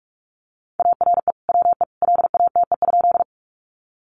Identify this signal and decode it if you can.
Morse code, heard as a keyed tone, about 44 wpm, 700 Hz. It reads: ARE WE LATE?
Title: Sentences from Top 300 Words 45wpm